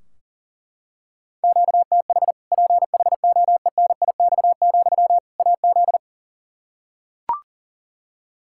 Morse code, heard as a keyed tone, about 39 wpm, 700 Hz. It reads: QTH PHOENIX, AZ
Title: Practice QSO - Exchange 1 - Beginning - 40wpm